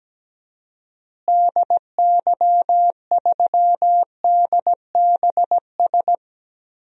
Morse code, heard as a keyed tone, about 17 wpm, 700 Hz. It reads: DY3DBS
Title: DXCC-100 Call Signs - 6 Characters - Repeated 2x - 17wpm